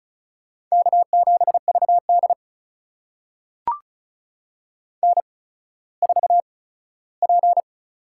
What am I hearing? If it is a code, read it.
Morse code, heard as a keyed tone, about 36 wpm, spaced as for 17 wpm, 700 Hz. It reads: K7VD N4P
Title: USA Call Signs - 8x Character Spacing - 35wpm